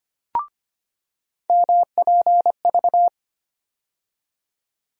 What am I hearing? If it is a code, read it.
Morse code, heard as a keyed tone, about 25 wpm, 700 Hz. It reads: MPV
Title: USA Call Sign Postfixes - 3 Letters 25wpm